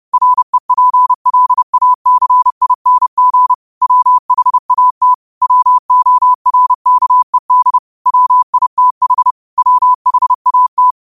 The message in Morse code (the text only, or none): REPLACING WHAT WORKED WITH WHAT